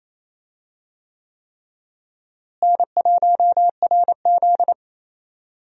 Morse code, heard as a keyed tone, about 28 wpm, 700 Hz. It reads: N1RZ
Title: USA Call Signs - 4 Characters - Repeated 3x - 28wpm